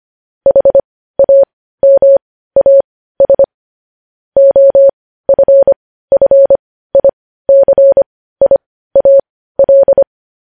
Morse code, heard as a keyed tone, about 25 wpm, spaced as for 15 wpm, 550 Hz. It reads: HAMAS OFFICIAL
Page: news-headlines-in-morse-code-at-15-wpm